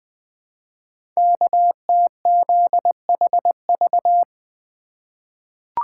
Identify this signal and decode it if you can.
Morse code, heard as a keyed tone, about 20 wpm, 700 Hz. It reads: KTZHV
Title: Code Groups - 5 Letters - 20wpm